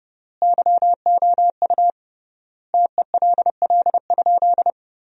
Morse code, heard as a keyed tone, about 30 wpm, 700 Hz. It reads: YOU TELL?
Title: 3x Word-Spacing - Sentences from Top 300 Words - 30wpm